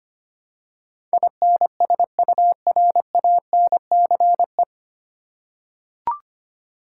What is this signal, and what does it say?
Morse code, heard as a keyed tone, about 25 wpm, 700 Hz. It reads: INSURANCE
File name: practice / 25wpm / 9 Letter Words - First 3 Letters Given - 25wpm